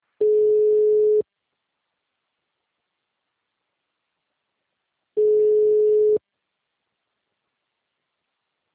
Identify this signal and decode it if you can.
Ringback tone (Europe, 425 Hz).